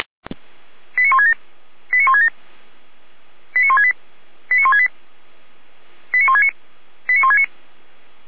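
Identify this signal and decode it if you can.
Five-tone selective call (ZVEI), heard as quick runs of five tones, tones of about 80 ms. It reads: 89167 89167 89168 89168 89169 89169
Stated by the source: ZVEI five-tone call received by SDR.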